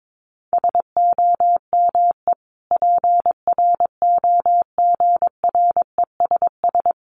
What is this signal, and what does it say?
Morse code, heard as a keyed tone, about 22 wpm, 700 Hz. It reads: SOME PROGRESS
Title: Sentences from Top 1400 Words 22wpm